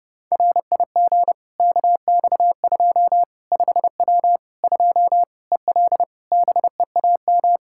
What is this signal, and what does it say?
Morse code, heard as a keyed tone, about 30 wpm, 700 Hz. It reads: RIG KX2 5W 2 EL BEAM
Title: Practice QSO - Exchange 2 - Info - 30wpm